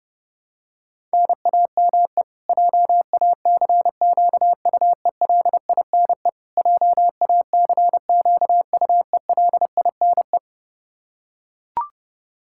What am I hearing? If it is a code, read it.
Morse code, heard as a keyed tone, about 32 wpm, 700 Hz. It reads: NAME JACQUELINE JACQUELINE
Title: Practice QSO - Exchange 1 - Beginning - 30wpm